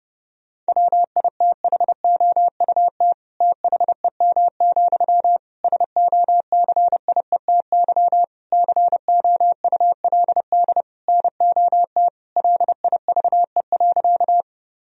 Morse code, heard as a keyed tone, about 30 wpm, 700 Hz. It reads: WITHOUT THEM, SOCIETY COULD NOT LIVE.
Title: Sentences from Top 1000 Words - Review Part 4 - 30wpm